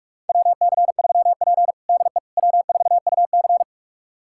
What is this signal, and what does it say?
Morse code, heard as a keyed tone, about 44 wpm, 690 Hz. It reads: WC3P DE W4UC
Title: USA Callsign DE Callsign - 4 Characters - 45wpm